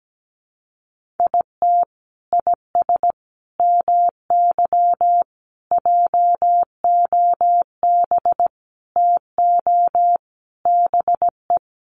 Morse code, heard as a keyed tone, about 17 wpm, 700 Hz. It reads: IT IS MY JOB TO BE
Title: Sentences from Top 600 Words 17wpm